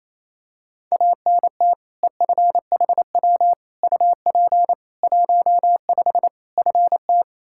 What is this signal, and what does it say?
Morse code, heard as a keyed tone, about 28 wpm, 700 Hz. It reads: ANT EFHW UP 15 FT
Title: Practice QSO - Exchange 2 - Info - 28wpm